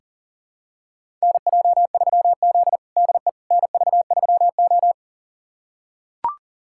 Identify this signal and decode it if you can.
Morse code, heard as a keyed tone, about 40 wpm, 700 Hz. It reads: NJ3Z DE NV3O
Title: USA Callsign DE Callsign - 4 Characters - 40wpm